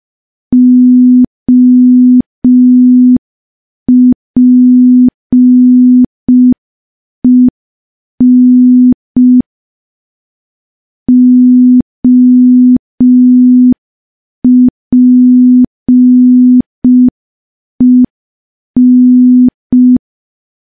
Morse code, heard as a keyed tone, about 5 wpm, 250 Hz. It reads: OPEN OPEN
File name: morse / morse-training-05WPM